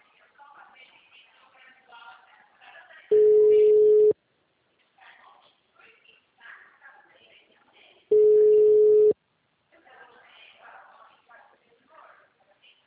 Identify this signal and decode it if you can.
Ringback tone (Europe, 425 Hz), with speech.
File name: out- -2050- - - .6